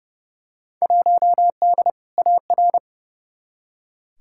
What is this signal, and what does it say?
Morse code, heard as a keyed tone, about 30 wpm, 700 Hz. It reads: 1D AR
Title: Field Day Exchanges - 30wpm